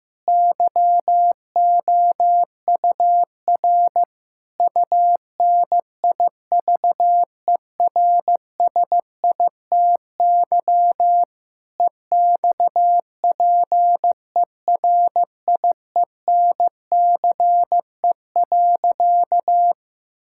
Morse code, heard as a keyed tone, about 15 wpm, 690 Hz. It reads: YOUR UNIVERSITY EXPERIENCE.
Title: Sentences from Top 1000 Words - Review Part 5 - 15wpm